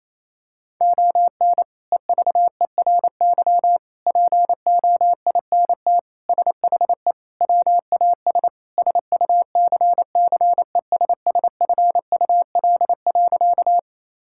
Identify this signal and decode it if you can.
Morse code, heard as a keyed tone, about 28 wpm, 700 Hz. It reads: ON EVERY POINT SHE WAS SUCCESSFUL.
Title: Sentences from Top 1000 Words - Review Part 6 - 28wpm